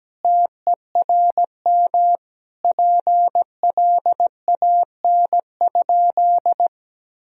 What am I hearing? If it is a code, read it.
Morse code, heard as a keyed tone, about 17 wpm, 700 Hz. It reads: TERM PLAN?
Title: Sentences from Top 1000 Words - Review Part 4 - 17wpm